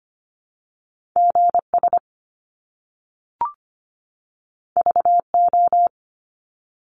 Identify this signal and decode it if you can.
Morse code, heard as a keyed tone, about 25 wpm, 700 Hz. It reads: GS VO
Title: Top Two Letter Combinations in Words 25wpm